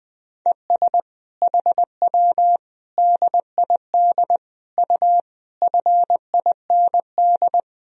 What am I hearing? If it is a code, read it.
Morse code, heard as a keyed tone, about 20 wpm, 700 Hz. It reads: ES HW DID U FIND
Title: Sentences from Top 100 Words and Abbreviations 20wpm